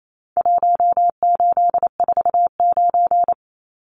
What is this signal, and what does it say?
Morse code, heard as a keyed tone, about 28 wpm, 700 Hz. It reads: 1849
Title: Numbers - Contest Style - 4-Digit 28wpm